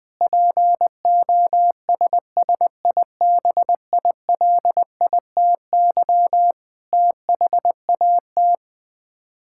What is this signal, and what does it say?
Morse code, heard as a keyed tone, about 21 wpm, 700 Hz. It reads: POSSIBILITY THAT
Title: Sets of 8 Words - 20wpm